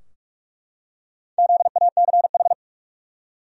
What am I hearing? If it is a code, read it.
Morse code, heard as a keyed tone, about 44 wpm, 700 Hz. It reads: ZACH
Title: USA Common Names - 45wpm